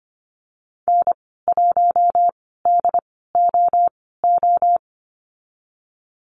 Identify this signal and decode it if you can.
Morse code, heard as a keyed tone, about 25 wpm, spaced as for 16 wpm, 700 Hz. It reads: N1DOO
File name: Lesson 15g - Number 5 - Call Signs 25wpm